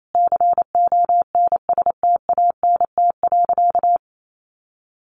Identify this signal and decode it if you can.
Morse code, heard as a keyed tone, about 28 wpm, 700 Hz. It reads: CONSTANT.
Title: Sentences from Top 1500 Words 28wpm